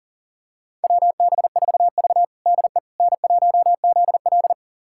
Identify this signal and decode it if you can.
Morse code, heard as a keyed tone, about 40 wpm, 700 Hz. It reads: WB4V DE N1ZL
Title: USA Callsign DE Callsign - 4 Characters - 40wpm